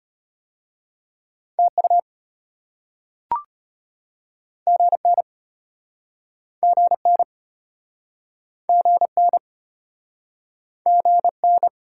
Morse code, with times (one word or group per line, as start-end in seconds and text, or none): 1.6-2.0 TU
4.7-5.2 GN
6.6-7.2 GN
8.7-9.4 GN
10.9-11.7 GN